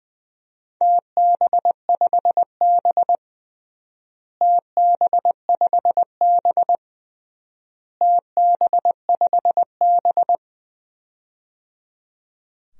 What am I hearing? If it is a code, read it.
Morse code, heard as a keyed tone, about 20 wpm, 700 Hz. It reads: TB5B TB5B TB5B
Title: DXCC-100 Call Signs - 4 Characters - Repeated 3x - 20wpm